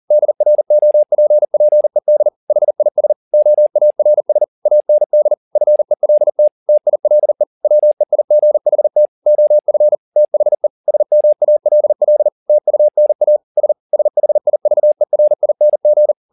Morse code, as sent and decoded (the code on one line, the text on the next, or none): -.. .-. --- .--. .--. -.. / .... .. ... / --- .- .-. ... / .- -. -.. / ..-. .-.. - / - .. .-.. / .-- . .. --. .... - / --- ..-. / - .... / ... -- .- .-.. .-.. / - ..- -. .- / ... / ... .... .. ...- . .-. .. -. --.
DROPPED HIS OARS AND FELT TILE WEIGHT OF THE SMALL TUNA S SHIVERING